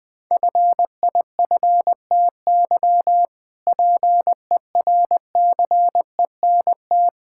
Morse code, heard as a keyed tone, about 20 wpm, 700 Hz. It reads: FIFTY PERCENT
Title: Sentences with 3-Word Phrases - Spoken Beginning - 20wpm